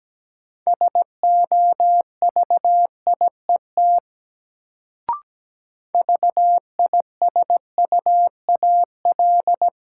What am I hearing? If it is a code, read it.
Morse code, heard as a keyed tone, about 17 wpm, 700 Hz. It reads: SOVIET VISUAL
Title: Sets of 5 Words Encore - New Word Drill - 17wpm